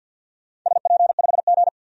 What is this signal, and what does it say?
Morse code, heard as a keyed tone, about 53 wpm, 700 Hz. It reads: SP5G
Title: DXCC-100 Call Signs - 4 Characters - 50wpm